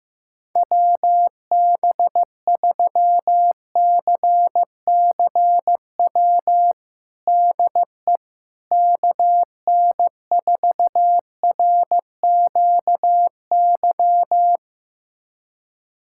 Morse code, heard as a keyed tone, about 15 wpm, 700 Hz. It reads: WB3CCW DE KN4RQY
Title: USA Callsign DE Callsign - 6 Characters - 15wpm